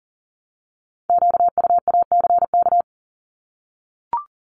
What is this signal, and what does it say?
Morse code, heard as a keyed tone, about 41 wpm, 690 Hz. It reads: QUACK
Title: Multi-letter Phonemes Repeated 3x 40wpm